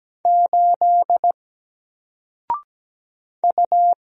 Morse code, heard as a keyed tone, about 17 wpm, 700 Hz. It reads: 8 U
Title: Single Letter-Number 17wpm